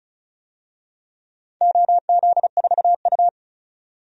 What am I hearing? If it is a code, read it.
Morse code, heard as a keyed tone, about 36 wpm, 700 Hz. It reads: OZ4U
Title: DXCC-100 Call Signs - 4 Characters - 35wpm